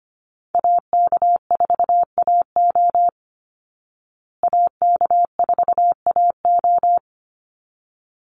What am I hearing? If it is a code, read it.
Morse code, heard as a keyed tone, about 25 wpm, 700 Hz. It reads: AK4AO AK4AO
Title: Call Signs Repeated 3x - 25wpm